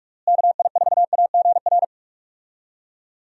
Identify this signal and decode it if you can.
Morse code, heard as a keyed tone, about 46 wpm, 700 Hz. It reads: KI4AGR
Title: USA Call Signs - 6 Characters - Repeated 2x - 45wpm